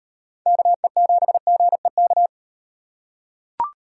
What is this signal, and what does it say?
Morse code, heard as a keyed tone, about 34 wpm, 700 Hz. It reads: KE7GEK